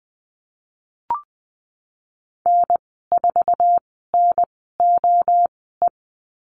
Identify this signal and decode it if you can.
Morse code, heard as a keyed tone, about 20 wpm, spaced as for 15 wpm, 700 Hz. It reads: N4NOE
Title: 048 Lesson 9g - Number 4 - Call Signs 20wpm